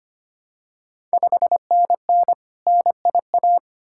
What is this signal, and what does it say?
Morse code, heard as a keyed tone, about 25 wpm, 690 Hz. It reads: 5NN NIA
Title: Ontario QSO Party Exchange 25wpm